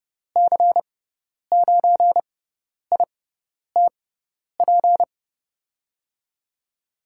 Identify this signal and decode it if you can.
Morse code, heard as a keyed tone, about 30 wpm, spaced as for 10 wpm, 700 Hz. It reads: C9ITP
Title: USA Call Signs - 6 Characters - 6x Character Spacing - 30wpm